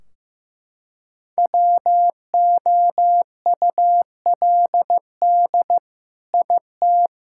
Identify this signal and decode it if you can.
Morse code, heard as a keyed tone, about 15 wpm, 700 Hz. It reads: WOULD IT
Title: Sentences from Top 1600 Words 15wpm